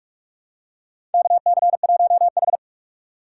Morse code, heard as a keyed tone, about 46 wpm, 700 Hz. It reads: KC1H
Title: USA Call Signs - 4 Characters - 2x Character Spacing - 45wpm